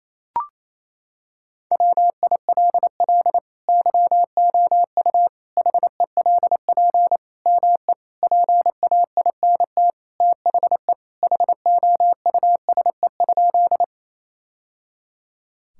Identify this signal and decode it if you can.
Morse code, heard as a keyed tone, about 28 wpm, 700 Hz. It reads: WILL YOU HELP ME PAINT THE HOUSE?
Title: Sentences from Top 1000 Words - Review Part 6 - 28wpm